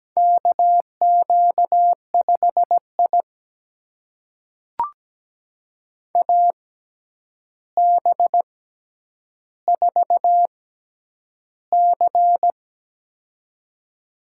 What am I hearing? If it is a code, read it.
Morse code, heard as a keyed tone, about 17 wpm, spaced as for 9 wpm, 700 Hz. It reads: KQ5I AB4C
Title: USA Call Signs - 4 Characters - 6x Character Spacing - 17wpm